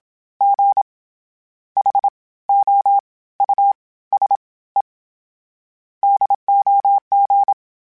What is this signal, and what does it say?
Morse code, heard as a keyed tone, about 27 wpm, 800 Hz. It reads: G H O U S E DOG
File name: General / General-mp3s / Houses @25